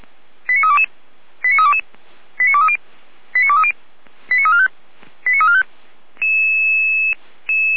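Five-tone selective call (ZVEI), heard as quick runs of five tones, tones of about 70 ms. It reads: 89322 89322 89230 89230 89345 89345
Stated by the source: ZVEI five-tone selective call received on an SDR by feuerwehr-orlamuende (fire brigade Orlamünde).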